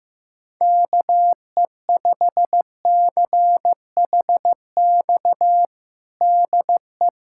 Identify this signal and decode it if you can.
Morse code, heard as a keyed tone, about 15 wpm, 700 Hz. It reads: KE5CHX DE
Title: USA Callsign DE Callsign - 6 Characters - 15wpm